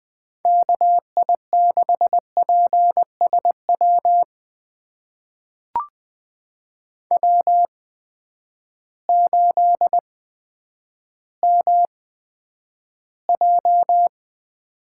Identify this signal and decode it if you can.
Morse code, heard as a keyed tone, about 20 wpm, spaced as for 10 wpm, 700 Hz. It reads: KI6PSW W8MJ
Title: USA Call Signs - 8x Character Spacing - 20wpm